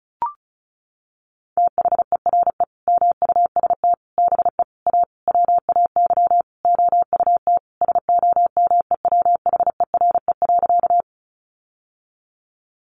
Morse code, heard as a keyed tone, about 34 wpm, 700 Hz. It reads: THERE MUST BE A WAY OUT SOMEWHERE.
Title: Sentences from Top 1400 Words 35wpm